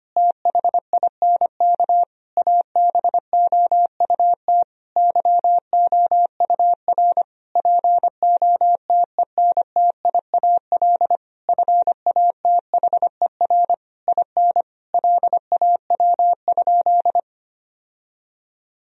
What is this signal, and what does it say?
Morse code, heard as a keyed tone, about 25 wpm, 700 Hz. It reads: THINK ABOUT YOUR POTENTIAL FATHER IN LAW?
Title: Sentences from Top 1000 Words - Review Part 4 - 25wpm